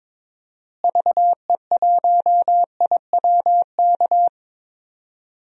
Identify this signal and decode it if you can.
Morse code, heard as a keyed tone, about 22 wpm, 700 Hz. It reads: VE1IWK
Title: DXCC-100 Call Signs - 6 Characters - Repeated 2x - 22wpm